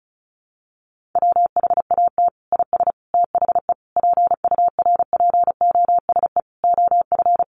Morse code, heard as a keyed tone, about 34 wpm, 700 Hz. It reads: WHAT IS THE PURPOSE OF
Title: Sentences from Top 1400 Words 35wpm